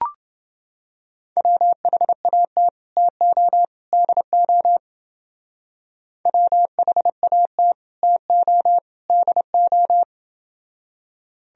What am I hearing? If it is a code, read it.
Morse code, heard as a keyed tone, about 29 wpm, 700 Hz. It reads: WHAT TO DO WHAT TO DO